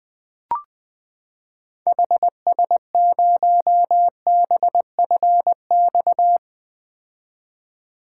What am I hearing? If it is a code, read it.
Morse code, heard as a keyed tone, about 20 wpm, 700 Hz. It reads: HS0BFX
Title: 2023 POTA Activator Call Signs - 20wpm